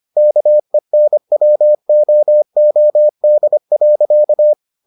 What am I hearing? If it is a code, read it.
Morse code, heard as a keyed tone, about 25 wpm, 600 Hz. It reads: KENWOOD.